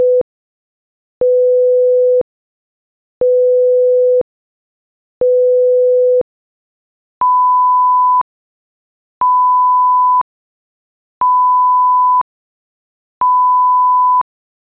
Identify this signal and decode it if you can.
There is no code.